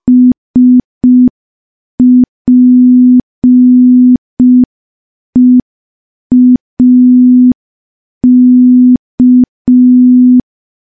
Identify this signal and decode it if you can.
Morse code, heard as a keyed tone, about 5 wpm, 250 Hz. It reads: SPEAK
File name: morse / morse-training-05WPM